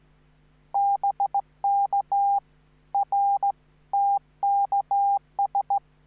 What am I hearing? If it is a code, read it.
Morse code, heard as a keyed tone, about 15 wpm, 800 Hz. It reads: BK R TKS